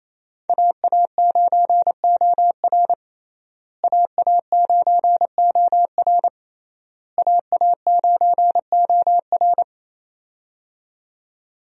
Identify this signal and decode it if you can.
Morse code, heard as a keyed tone, about 28 wpm, 700 Hz. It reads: AA9OR AA9OR AA9OR
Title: USA Call Signs - 5 Characters - Repeated 3x - 28wpm